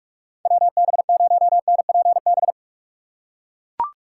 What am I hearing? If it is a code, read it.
Morse code, heard as a keyed tone, about 44 wpm, 700 Hz. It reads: WB0NPB
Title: USA Call Signs - 6x Character Spacing - 45wpm